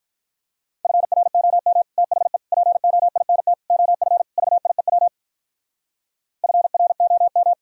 Morse code, heard as a keyed tone, about 52 wpm, 700 Hz. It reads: FROM THE POINT OF VIEW FROM